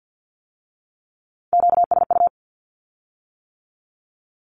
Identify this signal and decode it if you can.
Morse code, heard as a keyed tone, about 50 wpm, 700 Hz. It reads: QSU
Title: Q-Codes with Spoken Meaning - 50wpm